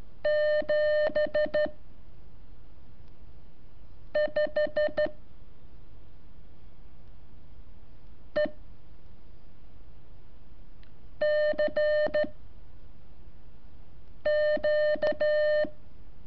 Morse code, heard as a keyed tone, about 12 wpm, 600 Hz. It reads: 7 5 E C Q
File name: curso / ejercicio 22